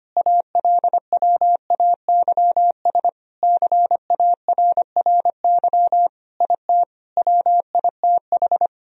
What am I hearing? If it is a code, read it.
Morse code, heard as a keyed tone, about 25 wpm, 700 Hz. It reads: ALWAYS CARRY IT WITH
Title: Sentences from Top 1000 Words - Review Part 1 - 25wpm